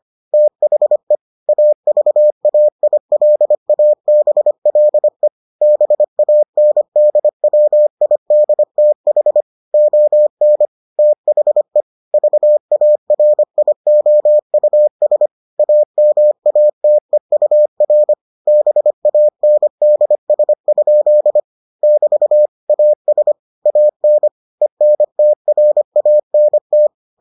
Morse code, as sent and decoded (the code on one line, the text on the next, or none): - .... / .- ...- .- .. .-.. .- -... .-.. / -... .- -. -.. .-- .. -.. - .... / --- -. / - .... / ...- .- .-. .. --- ..- ... / .- -- .- - . ..- .-. / -... .- -. -.. ... ..--.. / -...- / .- ... / .- -. / . -. - .-. .- -. -
THE AVAILABLE BANDWIDTH ON THE VARIOUS AMATEUR BANDS? <BT> AS AN ENTRANT